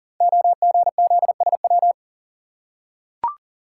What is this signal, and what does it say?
Morse code, heard as a keyed tone, about 38 wpm, 700 Hz. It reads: OGZSW
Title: Code Groups - 5 Letter:Numbers - 40wpm